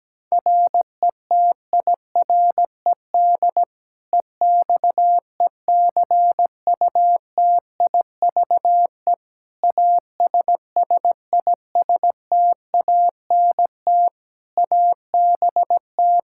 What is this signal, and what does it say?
Morse code, heard as a keyed tone, about 17 wpm, 700 Hz. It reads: RETIRED EXECUTIVE ASSISTANT ABT
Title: Practice QSO - Exchange 3 - More Info - 17wpm